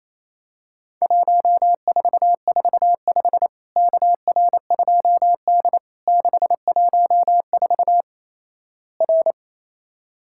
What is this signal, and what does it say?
Morse code, heard as a keyed tone, about 27 wpm, 700 Hz. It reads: 1445 KR2D 614 R
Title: Worked All Europe DX Contest - QTC Exchange - 28wpm